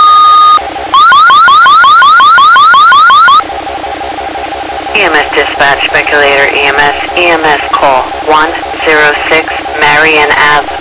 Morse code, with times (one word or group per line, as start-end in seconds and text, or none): none